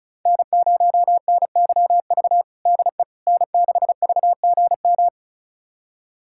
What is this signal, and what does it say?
Morse code, heard as a keyed tone, about 36 wpm, 700 Hz. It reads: N0NYV DE N6VGM
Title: USA Callsign DE Callsign - 5 Characters - 35wpm